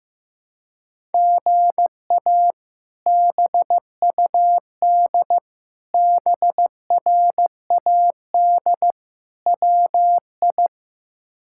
Morse code, heard as a keyed tone, about 15 wpm, 700 Hz. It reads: GA BUD BRAD WI
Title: K1USN SST QSO Practice - 15wpm